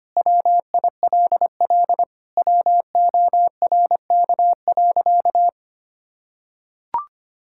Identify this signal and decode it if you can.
Morse code, heard as a keyed tone, about 25 wpm, 700 Hz. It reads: WILL WORK.
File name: practice / 25wpm / Sentences with 4-Word Phrases - 25wpm